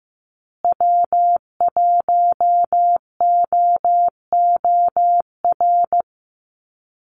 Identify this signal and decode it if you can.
Morse code, heard as a keyed tone, about 15 wpm, 700 Hz. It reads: W1OOR
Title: USA Call Signs Repeated 2x 15wpm